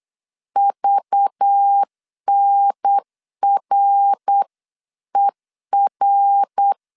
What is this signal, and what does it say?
Morse code, heard as a keyed tone, about 8 wpm, 800 Hz. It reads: VNRER
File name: CW / DET / 1 9 1500C S C 8wpm 5653